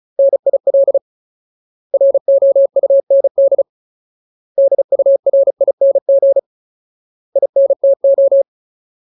Morse code, heard as a keyed tone, about 34 wpm, 550 Hz. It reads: NIL ROUND DURING INTO
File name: Ultimate Word List / 35WPM Ultimate